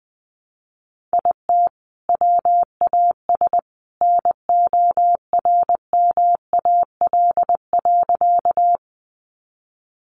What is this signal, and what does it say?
Morse code, heard as a keyed tone, about 20 wpm, 700 Hz. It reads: IT WAS NORMAL.
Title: Sentences from Top 1200 Words 20wpm